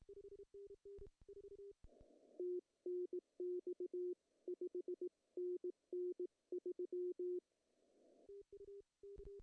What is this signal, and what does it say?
Morse code, heard as a keyed tone, about 18 wpm, 360 Hz. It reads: TNX 5NN 3
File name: CW / cqww-cw-k5tr